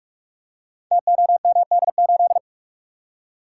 Morse code, heard as a keyed tone, about 44 wpm, 700 Hz. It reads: TOMD8
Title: Code Groups - 5 Letter:Numbers - 45wpm